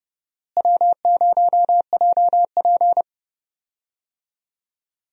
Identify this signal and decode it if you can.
Morse code, heard as a keyed tone, about 30 wpm, 700 Hz. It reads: W0JP
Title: USA Call Signs - 4 Characters - Repeated 2x - 30wpm